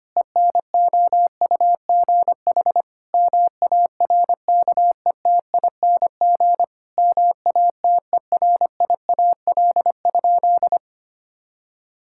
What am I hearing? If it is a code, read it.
Morse code, heard as a keyed tone, about 25 wpm, 700 Hz. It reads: ENOUGH MARKETING MATERIAL?
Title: Sentences from Top 1000 Words - Review Part 5 - 25wpm